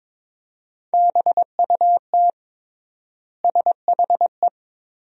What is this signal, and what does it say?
Morse code, heard as a keyed tone, about 21 wpm, 700 Hz. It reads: BUT SHE
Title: 3x Word-Spacing - Sentences from Top 300 Words - 22wpm